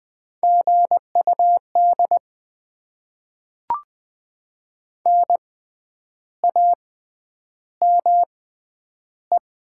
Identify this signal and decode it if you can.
Morse code, heard as a keyed tone, about 20 wpm, 700 Hz. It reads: G U D N A M E
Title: Top 20 QSO Elements - 6x Character Spacing - 20wpm